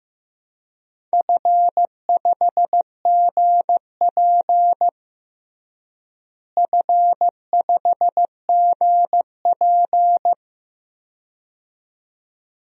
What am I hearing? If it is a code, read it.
Morse code, heard as a keyed tone, about 15 wpm, 700 Hz. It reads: F5GP F5GP
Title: DXCC-100 Call Signs - 4 Characters - Repeated 2x - 15wpm